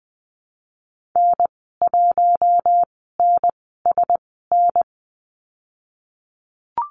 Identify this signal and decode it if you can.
Morse code, heard as a keyed tone, about 20 wpm, spaced as for 15 wpm, 700 Hz. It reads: N1NSN
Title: 048 Lesson 9g - Number 4 - Call Signs 20wpm